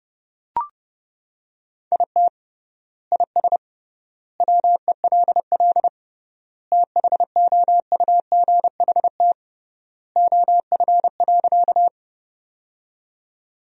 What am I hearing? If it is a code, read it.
Morse code, heard as a keyed tone, about 32 wpm, 700 Hz. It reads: IT IS WELL THOUGHT OF.
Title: 3x Word-Spacing - Sentences from Top 300 Words - 30wpm